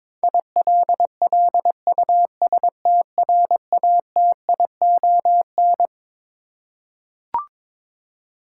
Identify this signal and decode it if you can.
Morse code, heard as a keyed tone, about 22 wpm, 700 Hz. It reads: ILLUSTRATION
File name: Twelve Letter Words - 22wpm